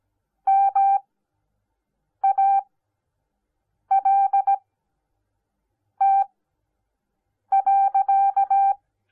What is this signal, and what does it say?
Morse code, heard as a keyed tone, about 17 wpm, spaced as for 6 wpm, 780 Hz. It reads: MALT.